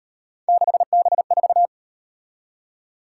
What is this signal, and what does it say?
Morse code, heard as a keyed tone, about 37 wpm, 700 Hz. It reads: 6B4